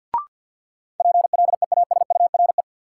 Morse code, heard as a keyed tone, about 52 wpm, 700 Hz. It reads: PLEASURE
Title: Sentences from Top 1600 Words - New Word Drill - 50wpm